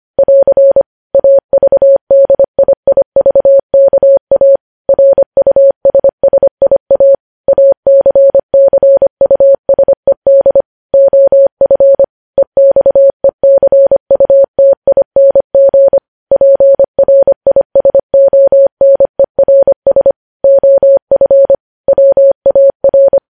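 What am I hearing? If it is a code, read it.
Morse code, heard as a keyed tone, about 25 wpm, 550 Hz. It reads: <AR> AVDIIVKA RUSSIA ACCUSED OF EXECUTING PRISONERS OF WAR
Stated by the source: Morse practice sending news headlines at 25 wpm.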